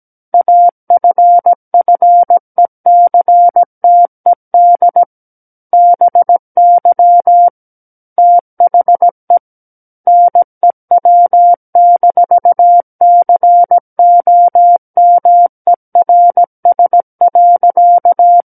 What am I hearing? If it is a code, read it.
Morse code, heard as a keyed tone, about 17 wpm, 700 Hz. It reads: AFFECTED BY THE NEW-COMERS.
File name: Book One / 08-Chapter 8 - 17 WPM